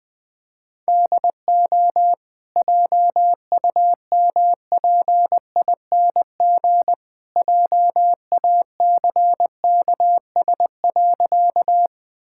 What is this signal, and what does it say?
Morse code, heard as a keyed tone, about 20 wpm, 700 Hz. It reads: DO JUMPING JACKS.
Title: Easy Sentences with Hints - 20wpm